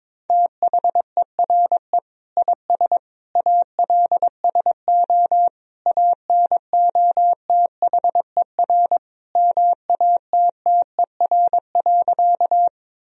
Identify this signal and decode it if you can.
Morse code, heard as a keyed tone, about 22 wpm, 700 Hz. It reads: THERE IS ALSO ANOTHER MATTER.
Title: Sentences from Top 1000 Words - Review Part 1 - 22wpm